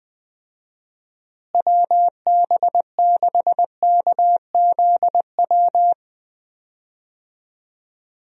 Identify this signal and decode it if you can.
Morse code, heard as a keyed tone, about 21 wpm, 700 Hz. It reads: WB6KZW
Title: USA Call Signs - 6 Characters - Repeated 3x - 20wpm